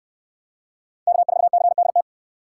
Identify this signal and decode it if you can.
Morse code, heard as a keyed tone, about 68 wpm, 700 Hz. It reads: K4CDT